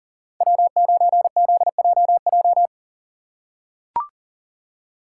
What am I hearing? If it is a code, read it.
Morse code, heard as a keyed tone, about 37 wpm, 700 Hz. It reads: W9ZJJ
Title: USA Call Signs - 5 Characters - 2x Character Spacing - 40wpm